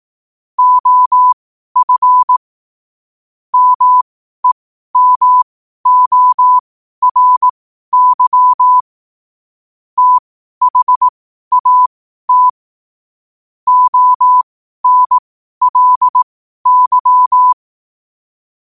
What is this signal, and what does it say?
Morse code, heard as a keyed tone, about 18 wpm, spaced as for 12 wpm, 1000 Hz. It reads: OF MEMORY THAT ONLY
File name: qotdfiles / 12wpm